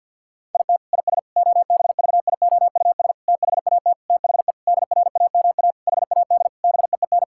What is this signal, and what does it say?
Morse code, heard as a keyed tone, about 50 wpm, 700 Hz. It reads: IT IS OBVIOUS THAT THE DRAMA HAD BEEN
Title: Sentences from Top 1600 Words 50wpm